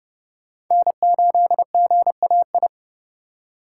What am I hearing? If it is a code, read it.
Morse code, heard as a keyed tone, about 31 wpm, 700 Hz. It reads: N8GAI
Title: USA Call Signs - 5 Characters - 4x Character Spacing - 30wpm